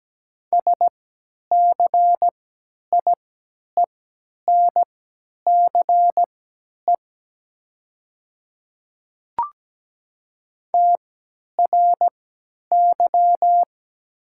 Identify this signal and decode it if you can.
Morse code, heard as a keyed tone, about 17 wpm, spaced as for 10 wpm, 700 Hz. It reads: SCIENCE TRY
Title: 3x Character Spacing - Top 500 Words - 17wpm